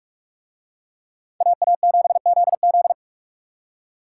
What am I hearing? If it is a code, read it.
Morse code, heard as a keyed tone, about 44 wpm, 700 Hz. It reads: AA7ZZ